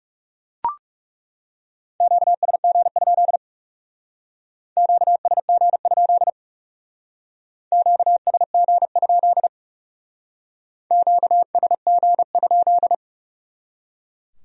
Morse code, heard as a keyed tone, about 38 wpm, 700 Hz. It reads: QSG? QSG? QSG? QSG?